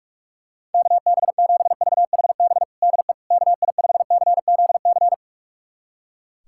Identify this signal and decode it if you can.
Morse code, heard as a keyed tone, about 42 wpm, 690 Hz. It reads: KB7VHB DE KI5CZC
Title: USA Callsign DE Callsign - 6 Characters - 45wpm